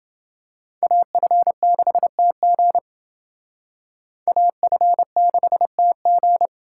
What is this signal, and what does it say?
Morse code, heard as a keyed tone, about 29 wpm, 700 Hz. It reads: AF6TG AF6TG